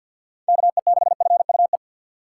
Morse code, heard as a keyed tone, about 53 wpm, 700 Hz. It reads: KE6FUE
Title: USA Call Signs - 6 Characters - 4x Character Spacing - 50wpm